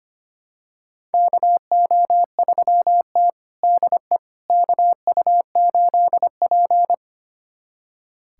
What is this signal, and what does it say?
Morse code, heard as a keyed tone, about 25 wpm, 700 Hz. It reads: KO3T DE KU8P
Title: USA Callsign DE Callsign - 4 Characters - 25wpm